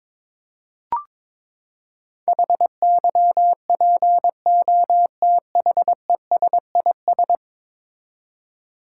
Morse code, heard as a keyed tone, about 22 wpm, 700 Hz. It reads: HYPOTHESIS
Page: morse-code-ninja-practice-22wpm